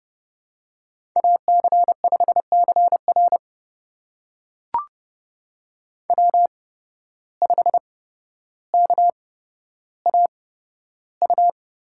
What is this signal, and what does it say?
Morse code, heard as a keyed tone, about 30 wpm, spaced as for 13 wpm, 700 Hz. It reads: AC5CR W5KAU
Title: USA Call Signs - 5 Characters - 8x Character Spacing - 30wpm